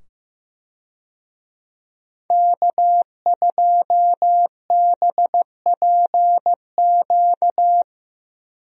Morse code, heard as a keyed tone, about 15 wpm, 700 Hz. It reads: K2BPQ
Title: USA Call Signs - 5 Characters - 15wpm